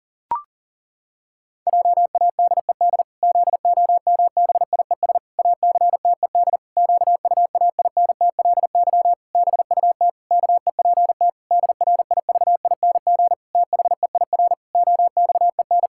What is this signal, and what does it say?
Morse code, heard as a keyed tone, about 40 wpm, 700 Hz. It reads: JADED ZOMBIES ACTED QUAINTLY BUT KEPT DRIVING THEIR OXEN
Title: Pangram Challenge v2 - 40wpm